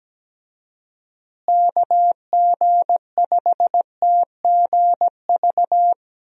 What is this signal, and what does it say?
Morse code, heard as a keyed tone, about 17 wpm, 700 Hz. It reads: KG5TGV
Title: USA Call Signs - 6 Characters - Repeated 3x - 17wpm